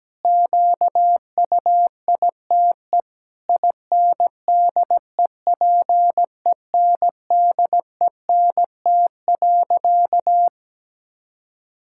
Morse code, heard as a keyed tone, about 17 wpm, 700 Hz. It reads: QUITE INDEPENDENT.
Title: Sentences from Top 1000 Words - Review Part 6 - 17wpm